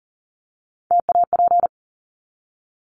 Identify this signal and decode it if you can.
Morse code, heard as a keyed tone, about 40 wpm, 700 Hz. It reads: TAP
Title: Top 1251-1500 Words 40wpm